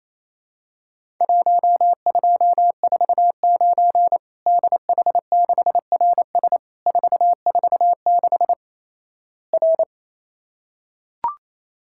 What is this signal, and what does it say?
Morse code, heard as a keyed tone, about 28 wpm, 700 Hz. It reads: 1249 DH6RS 446 R
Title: Worked All Europe DX Contest - QTC Exchange - 28wpm